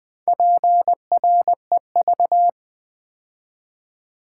Morse code, heard as a keyed tone, about 20 wpm, 700 Hz. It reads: PREV
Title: Build A Word - 4 Letters At A Time - 10-Letter Words - 20wpm